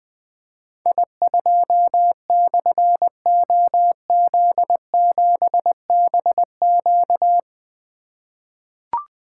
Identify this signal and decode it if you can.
Morse code, heard as a keyed tone, about 20 wpm, 700 Hz. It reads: I2/OZ7BQ
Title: Difficult Call Signs 20wpm